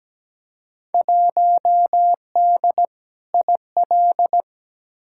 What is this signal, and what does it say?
Morse code, heard as a keyed tone, about 17 wpm, 700 Hz. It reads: 1D IL
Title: Field Day Exchanges - 17wpm